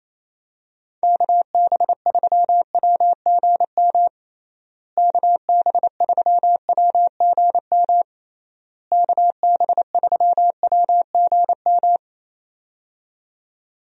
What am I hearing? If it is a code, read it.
Morse code, heard as a keyed tone, about 28 wpm, 700 Hz. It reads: KB3WGM KB3WGM KB3WGM
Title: USA Call Signs - 6 Characters - Repeated 3x - 28wpm